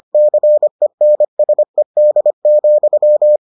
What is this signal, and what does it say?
Morse code, heard as a keyed tone, about 25 wpm, 600 Hz. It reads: CENSED,